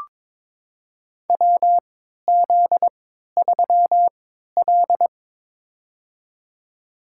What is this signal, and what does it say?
Morse code, heard as a keyed tone, about 22 wpm, spaced as for 12 wpm, 700 Hz. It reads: WZ3L
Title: USA Call Signs - 4 Characters - 3x Character Spacing - 22wpm